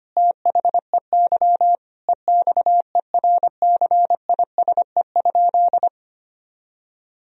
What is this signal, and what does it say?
Morse code, heard as a keyed tone, about 25 wpm, 700 Hz. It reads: THEY EXERCISE?
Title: Sentences from Top 1000 Words - Review Part 4 - 25wpm